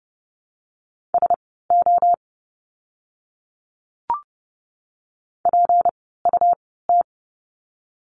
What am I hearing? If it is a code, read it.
Morse code, heard as a keyed tone, about 30 wpm, spaced as for 17 wpm, 700 Hz. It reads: SO PUT
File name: course / 30:10wpm / 230 Lesson 37f - Letter x - Words 30wpm